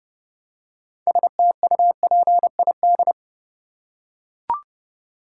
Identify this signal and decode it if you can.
Morse code, heard as a keyed tone, about 30 wpm, 700 Hz. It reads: STUPID
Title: Six Letter Words - 30wpm